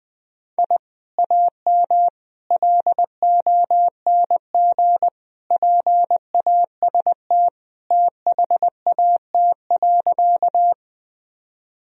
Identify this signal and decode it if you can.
Morse code, heard as a keyed tone, about 19 wpm, 700 Hz. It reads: I AM LONG PAST THAT.
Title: Sentences from Top 1000 Words - Review Part 3 - 20wpm